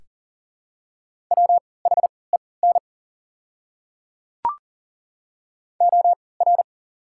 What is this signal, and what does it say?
Morse code, heard as a keyed tone, about 40 wpm, spaced as for 23 wpm, 690 Hz. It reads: WHEN OR
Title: 3x Character Spacing - Top 500 Words - 40wpm